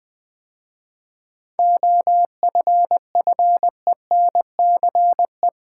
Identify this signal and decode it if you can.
Morse code, heard as a keyed tone, about 20 wpm, 700 Hz. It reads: OFFENCE
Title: 7 Letter Words - First 3 Letters Given - 20wpm